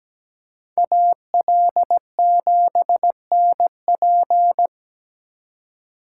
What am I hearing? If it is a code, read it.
Morse code, heard as a keyed tone, about 20 wpm, 700 Hz. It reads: AL7NP
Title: Speed Racing - USA Call Signs 17wpm